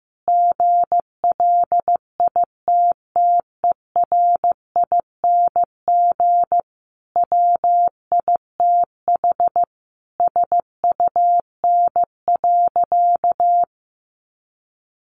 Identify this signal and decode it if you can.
Morse code, heard as a keyed tone, about 15 wpm, 700 Hz. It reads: GLITTERING WITH SUN.
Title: A Sound Of Thunder 15wpm